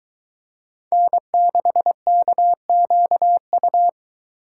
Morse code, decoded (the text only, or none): N6KQU